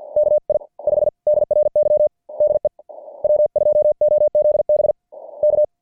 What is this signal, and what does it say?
Morse code, heard as a keyed tone, about 50 wpm, spaced as for 41 wpm, 590 Hz. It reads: KI5DRY DE W2OZB K
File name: archived / CWBros